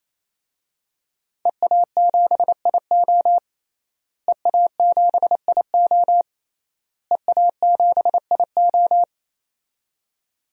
Morse code, heard as a keyed tone, about 27 wpm, 700 Hz. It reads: EA7IO EA7IO EA7IO
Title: DXCC-100 Call Signs - 5 Characters - Repeated 3x - 28wpm